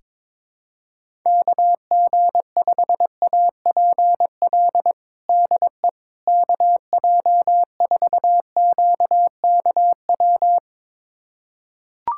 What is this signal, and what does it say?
Morse code, heard as a keyed tone, about 22 wpm, 700 Hz. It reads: KG5APL DE KJ4QKW
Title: USA Callsign DE Callsign 22wpm